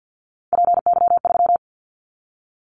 Morse code, heard as a keyed tone, about 50 wpm, 700 Hz. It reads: PY3
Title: Callsign Prefixes for Top 50 DXCC - 50wpm